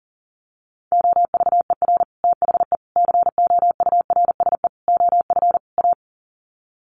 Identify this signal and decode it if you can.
Morse code, heard as a keyed tone, about 40 wpm, 700 Hz. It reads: OVER THE COURSE OF A
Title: Sets of 5 Words Encore 40wpm